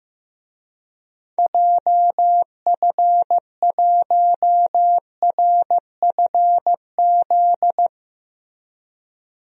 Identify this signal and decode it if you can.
Morse code, heard as a keyed tone, about 15 wpm, 700 Hz. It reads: JF1RFZ
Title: DXCC-100 Call Signs - 6 Characters - Repeated 2x - 15wpm